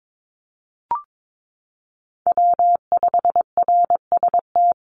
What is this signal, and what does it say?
Morse code, heard as a keyed tone, about 22 wpm, 700 Hz. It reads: W5RST
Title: NA SOTA Chaser Callsigns 22wpm